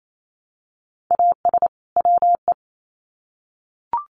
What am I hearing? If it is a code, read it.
Morse code, heard as a keyed tone, about 28 wpm, 700 Hz. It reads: AS WE
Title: Sets of 2 Words 28wpm